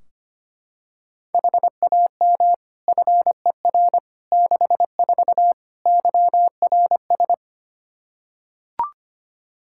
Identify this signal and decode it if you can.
Morse code, heard as a keyed tone, about 25 wpm, 700 Hz. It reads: HAM FER 64 YRS
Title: Practice QSO - Exchange 3 - More Info - 25wpm